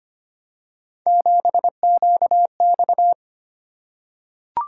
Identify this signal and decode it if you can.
Morse code, heard as a keyed tone, about 25 wpm, 700 Hz. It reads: 7QX
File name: Three Letter-Number Combinations - 25wpm